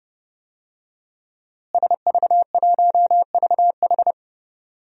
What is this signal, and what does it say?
Morse code, heard as a keyed tone, about 30 wpm, 700 Hz. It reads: SV1VH
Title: DXCC-100 Call Signs - 5 Characters - Repeated 2x - 30wpm